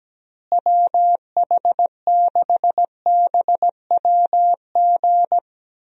Morse code, heard as a keyed tone, about 17 wpm, 700 Hz. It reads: WH6BWG
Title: USA Call Signs - 6 Characters - Repeated 3x - 17wpm